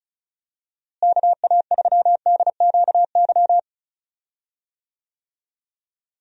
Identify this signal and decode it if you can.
Morse code, heard as a keyed tone, about 34 wpm, 700 Hz. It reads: KA3DQY
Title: USA Call Signs - 6 Characters - Repeated 3x - 35wpm